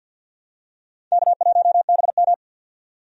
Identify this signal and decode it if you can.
Morse code, heard as a keyed tone, about 52 wpm, 700 Hz. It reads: K1BM
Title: USA Call Signs - 4 Characters - 50wpm